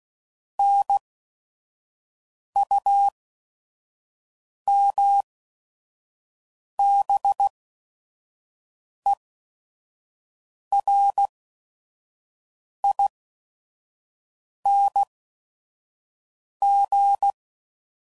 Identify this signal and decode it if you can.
Morse code, heard as a keyed tone, about 16 wpm, spaced as for 5 wpm, 800 Hz. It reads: NUMBERING
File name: Suffixes / Special ING's / VerySlowToFaster ING's